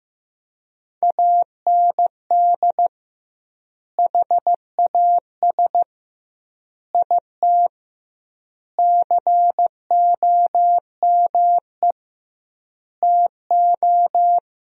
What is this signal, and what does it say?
Morse code, heard as a keyed tone, about 15 wpm, 700 Hz. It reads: AND HAS IT COME TO
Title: 2x Word-Spacing - Sentences from Top 100 Words - 15wpm